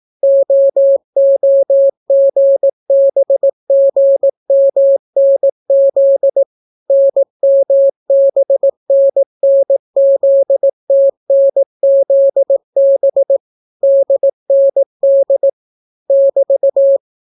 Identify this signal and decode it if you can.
Morse code, heard as a keyed tone, about 18 wpm, 550 Hz. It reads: OOGBGMNZ NMBNNZTNZB DND <BT>